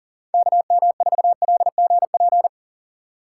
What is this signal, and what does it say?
Morse code, heard as a keyed tone, about 38 wpm, 700 Hz. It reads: KM4LGP
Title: USA Call Signs - 2x Character Spacing - 40wpm